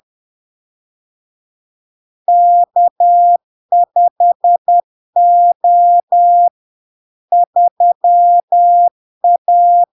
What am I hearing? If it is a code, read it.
Morse code, heard as a keyed tone, about 10 wpm, 700 Hz. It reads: K5O 3A
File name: LARK-FD-2022-Log-in-CW-10wpm-700-Hz